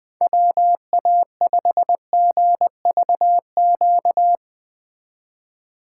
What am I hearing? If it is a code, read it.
Morse code, heard as a keyed tone, about 19 wpm, 700 Hz. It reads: WA5GVQ
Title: Speed Racing - USA 2x3 Call Signs 17wpm